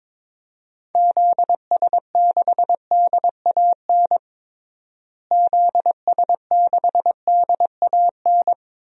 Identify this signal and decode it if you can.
Morse code, heard as a keyed tone, about 22 wpm, 700 Hz. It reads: ZS6DAN ZS6DAN
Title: DXCC-100 Call Signs - 6 Characters - Repeated 3x - 22wpm